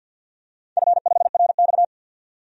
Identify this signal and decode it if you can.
Morse code, heard as a keyed tone, about 51 wpm, 700 Hz. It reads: F5RX
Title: DXCC-100 Call Signs - 4 Characters - 50wpm